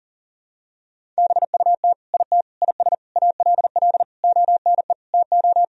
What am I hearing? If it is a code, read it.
Morse code, heard as a keyed tone, about 40 wpm, 700 Hz. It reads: BUT IT IS ALL ONE TO
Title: Sentences from Top 1000 Words - Review Part 1 - 40wpm